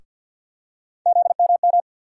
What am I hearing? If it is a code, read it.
Morse code, heard as a keyed tone, about 49 wpm, 700 Hz. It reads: ZMM
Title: USA Call Sign Postfixes - 3 Letters 50wpm